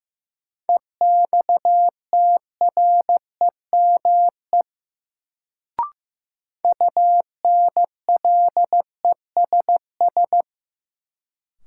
Morse code, heard as a keyed tone, about 15 wpm, 700 Hz. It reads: EXTREME UNLESS
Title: Sentences from Top 1300 Words - New Word Drill Alt - 15wpm